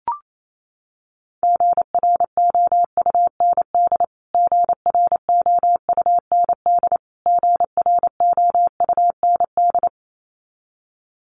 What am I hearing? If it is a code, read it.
Morse code, heard as a keyed tone, about 28 wpm, 700 Hz. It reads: GROUND GROUND GROUND
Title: Top 500 Words Repeated 28wpm